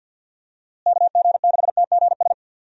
Morse code, heard as a keyed tone, about 52 wpm, 700 Hz. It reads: KG6TGS
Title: USA Call Signs - 6 Characters - Repeated 2x - 50wpm